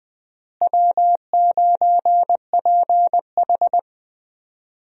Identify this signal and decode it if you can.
Morse code, heard as a keyed tone, about 21 wpm, 700 Hz. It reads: W9PH
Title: USA Call Signs - 4 Characters - Repeated 3x - 20wpm